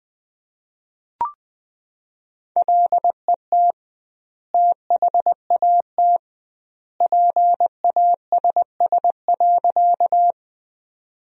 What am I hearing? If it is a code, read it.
Morse code, heard as a keyed tone, about 20 wpm, 700 Hz. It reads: LET THAT PASS.
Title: 2x Word-Spacing - Sentences from Top 400 Words - 20wpm